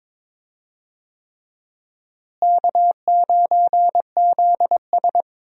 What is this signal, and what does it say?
Morse code, heard as a keyed tone, about 22 wpm, 700 Hz. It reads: K9ZS
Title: USA Call Signs - 4 Characters - Repeated 3x - 22wpm